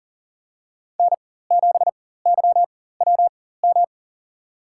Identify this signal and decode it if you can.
Morse code, heard as a keyed tone, about 40 wpm, spaced as for 19 wpm, 700 Hz. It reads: N7YWM
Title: USA Call Signs - 4x Character Spacing - 40wpm